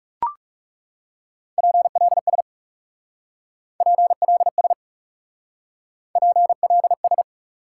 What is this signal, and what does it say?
Morse code, heard as a keyed tone, about 37 wpm, 700 Hz. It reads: PLS PLS PLS
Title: Speed-Racing Top 90 QSO Elements 30wpm